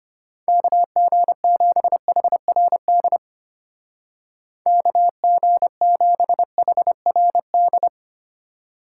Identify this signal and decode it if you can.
Morse code, heard as a keyed tone, about 32 wpm, 700 Hz. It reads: KG7HRD KG7HRD